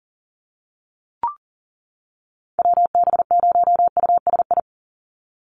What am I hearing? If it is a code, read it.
Morse code, heard as a keyed tone, about 40 wpm, 700 Hz. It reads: WB0USI
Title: NA SOTA Chaser Callsigns 40wpm